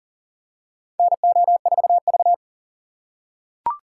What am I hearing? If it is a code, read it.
Morse code, heard as a keyed tone, about 40 wpm, 700 Hz. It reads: NO4V
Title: USA Call Signs - 2x Character Spacing - 40wpm